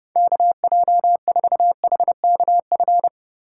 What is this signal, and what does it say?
Morse code, heard as a keyed tone, about 36 wpm, 700 Hz. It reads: KJ4HKF
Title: Speed Racing - USA 2x3 Call Signs 30wpm